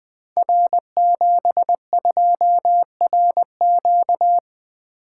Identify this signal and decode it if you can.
Morse code, heard as a keyed tone, about 20 wpm, 700 Hz. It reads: R72RQ
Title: Code Groups - 5 Letter:Numbers - 20wpm